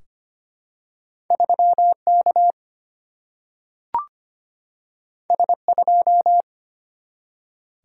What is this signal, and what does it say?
Morse code, heard as a keyed tone, about 25 wpm, 700 Hz. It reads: 3K S2
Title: Single-Letter Single-Number 25wpm